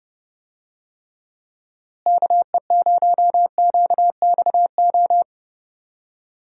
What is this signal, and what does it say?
Morse code, heard as a keyed tone, about 32 wpm, 700 Hz. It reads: KE0QXO
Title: USA Call Signs - 6 Characters - Repeated 3x - 30wpm